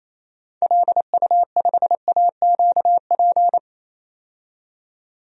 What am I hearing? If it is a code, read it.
Morse code, heard as a keyed tone, about 28 wpm, 700 Hz. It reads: LU5AQP
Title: DXCC-100 Call Signs - 6 Characters - 28wpm